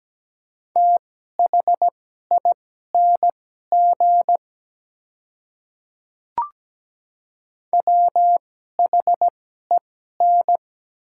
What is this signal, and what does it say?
Morse code, heard as a keyed tone, about 17 wpm, spaced as for 12 wpm, 700 Hz. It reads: THING WHEN
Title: 2x Character Spacing - Top 200 Words - 17wpm